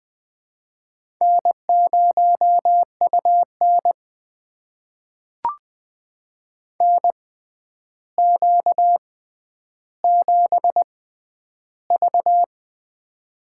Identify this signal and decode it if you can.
Morse code, heard as a keyed tone, about 20 wpm, spaced as for 10 wpm, 700 Hz. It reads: N0UN NQ7V
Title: USA Call Signs - 4 Characters - 6x Character Spacing - 20wpm